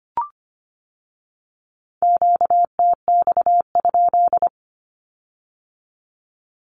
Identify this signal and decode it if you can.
Morse code, heard as a keyed tone, about 25 wpm, 700 Hz. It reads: QTX?
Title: Q-Codes with Spoken Meaning - 25wpm